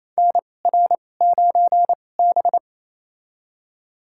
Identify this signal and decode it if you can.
Morse code, heard as a keyed tone, about 28 wpm, spaced as for 20 wpm, 700 Hz. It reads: NR9B
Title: USA Call Signs - 4 Characters - 2x Character Spacing - 28wpm